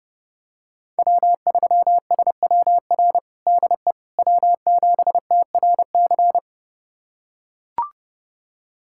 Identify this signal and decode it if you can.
Morse code, heard as a keyed tone, about 30 wpm, 700 Hz. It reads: W3SWR DE W7TRC
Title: USA Callsign DE Callsign - 5 Characters - 30wpm